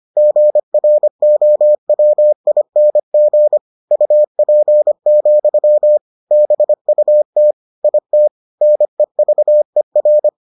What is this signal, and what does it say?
Morse code, heard as a keyed tone, about 25 wpm, 600 Hz. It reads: GROWING UP, BUT IT NEVER